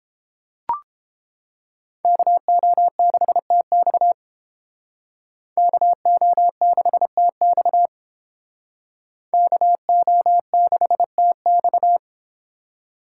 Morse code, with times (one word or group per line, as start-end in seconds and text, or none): 2.0-4.1 KO6TX
5.6-7.9 KO6TX
9.3-12.0 KO6TX